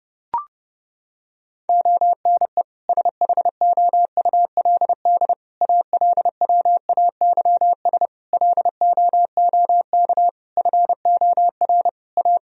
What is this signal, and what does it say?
Morse code, heard as a keyed tone, about 30 wpm, 700 Hz. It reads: ONE SHOULD ALWAYS LOOK FOR A
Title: Sentences from Top 1000 Words - Review Part 4 - 30wpm